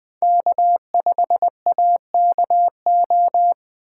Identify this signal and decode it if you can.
Morse code, heard as a keyed tone, about 20 wpm, 700 Hz. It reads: K5AKO
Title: USA Call Signs - 5 Characters - 4x Character Spacing - 20wpm